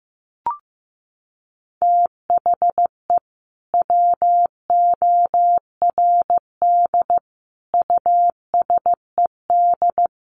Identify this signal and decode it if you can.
Morse code, heard as a keyed tone, about 15 wpm, 700 Hz. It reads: THE WORD USED
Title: Sentences from Top 1100 Words 15wpm